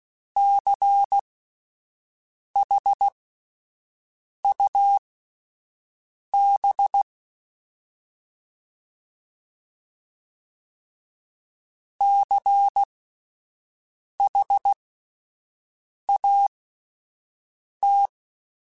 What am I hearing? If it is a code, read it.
Morse code, heard as a keyed tone, about 16 wpm, spaced as for 5 wpm, 800 Hz. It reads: CHUB CHAT
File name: Spaced Alphas / Spaced Alphas mp3s / Cs sp @10